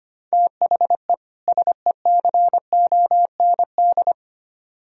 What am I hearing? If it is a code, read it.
Morse code, heard as a keyed tone, about 33 wpm, 700 Hz. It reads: THE SECOND